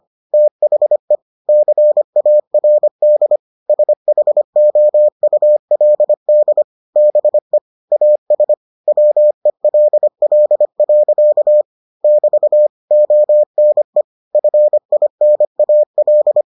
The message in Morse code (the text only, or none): THE CARD SHOULD BE AS WELL. <BT> ONE FINAL